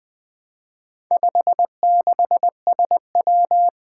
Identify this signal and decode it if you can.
Morse code, heard as a keyed tone, about 20 wpm, 700 Hz. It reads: 56SW
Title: Four Challenging Letter-Number Combinations 20wpm